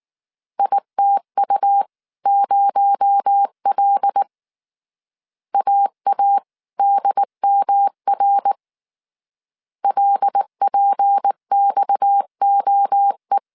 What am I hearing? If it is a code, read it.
Morse code, heard as a keyed tone, about 19 wpm, 800 Hz. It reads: ITU0L AADMR LPXOE
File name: CW / DET / 1 6 1500C S C 18wpm 2356